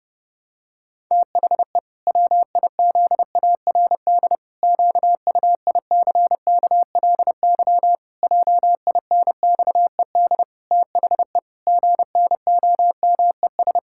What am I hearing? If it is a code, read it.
Morse code, heard as a keyed tone, about 32 wpm, 700 Hz. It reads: THE WIZARD QUICKLY JINXED THE GNOMES
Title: Pangram Challenge v2 - 30wpm